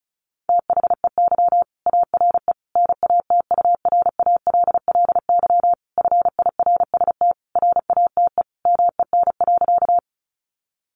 Morse code, heard as a keyed tone, about 34 wpm, 700 Hz. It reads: THEY ARE NATURALLY FIRST RATE MEN.
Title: Sentences from Top 1300 Words 35wpm